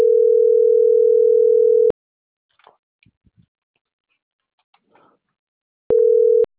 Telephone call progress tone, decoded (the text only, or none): ringback tone